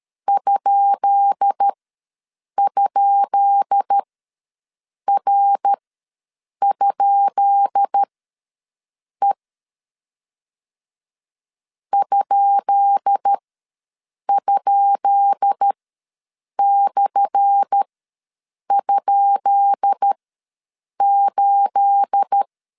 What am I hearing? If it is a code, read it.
Morse code, heard as a keyed tone, about 13 wpm, spaced as for 7 wpm, 800 Hz. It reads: ??R?E ??/?8
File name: LEZIONI / 10 SIGNS / 10B INT ws 12wpm 3527